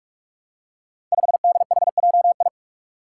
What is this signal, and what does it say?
Morse code, heard as a keyed tone, about 44 wpm, 700 Hz. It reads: 5DHJI
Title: Five Challenging Letter-Number Combinations 45wpm